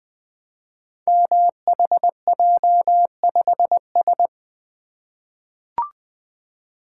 Morse code, heard as a keyed tone, about 20 wpm, 700 Hz. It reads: MHJ5S
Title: Code Groups - 5 Letter:Numbers - 20wpm